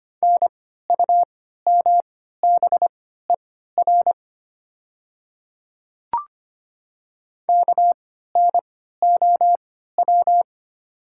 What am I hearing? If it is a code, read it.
Morse code, heard as a keyed tone, about 25 wpm, spaced as for 14 wpm, 700 Hz. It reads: NUMBER KNOW
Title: 3x Character Spacing - Top 200 Words - 25wpm